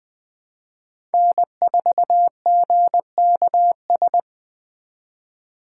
Morse code, heard as a keyed tone, about 20 wpm, 700 Hz. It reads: N4GKS
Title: USA Call Signs - 5 Characters - 4x Character Spacing - 20wpm